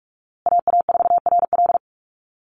Morse code, heard as a keyed tone, about 44 wpm, 700 Hz. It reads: AA4RL
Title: USA Call Signs Repeated 2x 45wpm